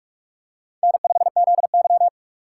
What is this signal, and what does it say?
Morse code, heard as a keyed tone, about 46 wpm, 700 Hz. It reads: N5ZY
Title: USA Call Signs - 4 Characters - 6x Character Spacing - 45wpm